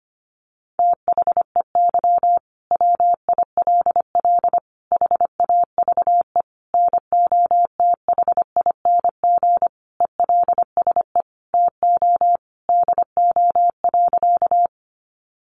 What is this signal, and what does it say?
Morse code, heard as a keyed tone, about 25 wpm, 700 Hz. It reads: THEY WILL HAVE NOTHING ELSE TO DO.
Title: Sentences from Top 800 Words 25wpm